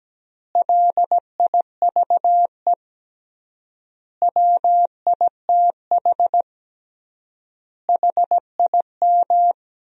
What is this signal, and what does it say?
Morse code, heard as a keyed tone, about 17 wpm, 700 Hz. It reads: LIVE WITH HIM
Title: 3x Word-Spacing - Sentences from Top 500 Words - 17wpm